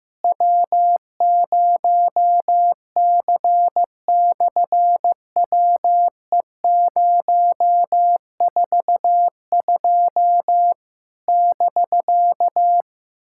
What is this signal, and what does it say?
Morse code, heard as a keyed tone, about 15 wpm, 700 Hz. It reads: W0C/WE042 <BK>
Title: SOTA QSO Practice 15wpm